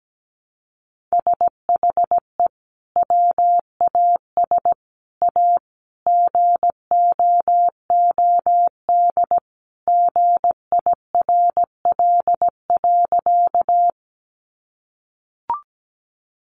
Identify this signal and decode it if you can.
Morse code, heard as a keyed tone, about 17 wpm, 700 Hz. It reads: SHE WAS A GOOD GIRL.
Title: Sentences from Top 400 Words 17wpm